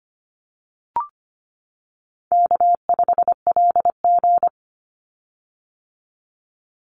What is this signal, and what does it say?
Morse code, heard as a keyed tone, about 25 wpm, 700 Hz. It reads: K5LG
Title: Easy Call Signs - 25wpm